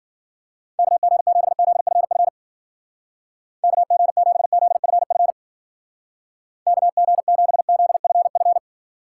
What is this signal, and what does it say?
Morse code, heard as a keyed tone, about 52 wpm, 700 Hz. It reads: KG7ZFF KG7ZFF KG7ZFF